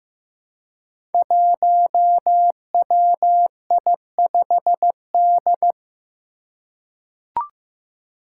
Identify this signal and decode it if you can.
Morse code, heard as a keyed tone, about 15 wpm, 700 Hz. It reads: 1WI5D
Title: Five Challenging Letter-Number Combinations 15wpm